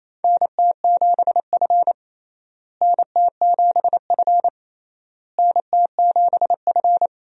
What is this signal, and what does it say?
Morse code, heard as a keyed tone, about 28 wpm, 700 Hz. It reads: NT7F NT7F NT7F
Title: USA Call Signs - 4 Characters - Repeated 3x - 28wpm